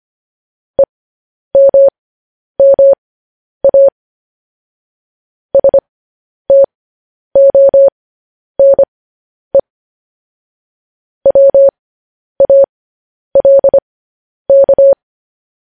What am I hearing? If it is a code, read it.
Morse code, heard as a keyed tone, about 25 wpm, spaced as for 10 wpm, 550 Hz. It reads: EMMA STONE WALK